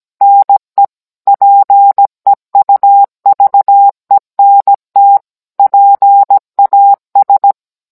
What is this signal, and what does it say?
Morse code, heard as a keyed tone, about 17 wpm, 800 Hz. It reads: NE PEUVENT PAS